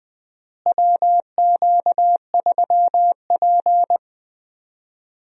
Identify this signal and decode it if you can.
Morse code, heard as a keyed tone, about 20 wpm, 700 Hz. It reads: WQ3P
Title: USA Call Signs - 4 Characters - Repeated 3x - 20wpm